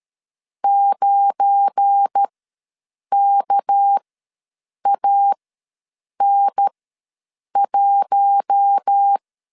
Morse code, heard as a keyed tone, about 13 wpm, spaced as for 7 wpm, 800 Hz. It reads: 9KAN1
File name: LEZIONI / 7 GF9Z / 7 GF9 ws 12wpm 3344